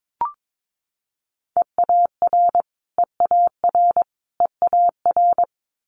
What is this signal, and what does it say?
Morse code, heard as a keyed tone, about 22 wpm, 700 Hz. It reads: EAR EAR EAR
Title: Multi-letter Phonemes Repeated 3x 22wpm